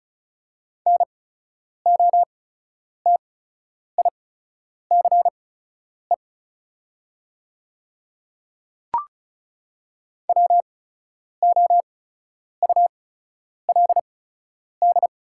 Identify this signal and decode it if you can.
Morse code, heard as a keyed tone, about 36 wpm, spaced as for 10 wpm, 690 Hz. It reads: NOTICE WOULD
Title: 8x Character Spacing - Top 500 Words - 35wpm